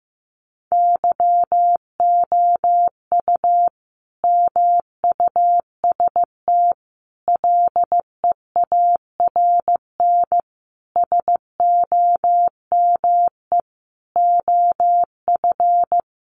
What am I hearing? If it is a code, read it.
Morse code, heard as a keyed tone, about 14 wpm, 700 Hz. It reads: YOU MUST LEARN SOME OF
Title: Sentences from Top 900 Words 15wpm